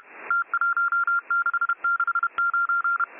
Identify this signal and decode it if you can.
Morse code, heard as a keyed tone, about 31 wpm, 1390 Hz. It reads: T1669